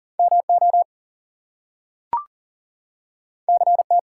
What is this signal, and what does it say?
Morse code, heard as a keyed tone, about 40 wpm, 700 Hz. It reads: MO CT
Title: US States and Canadian Provinces 40wpm